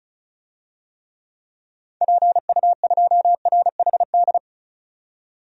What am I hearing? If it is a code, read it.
Morse code, heard as a keyed tone, about 36 wpm, 700 Hz. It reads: PU2RHD
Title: DXCC-100 Call Signs - 6 Characters - 35wpm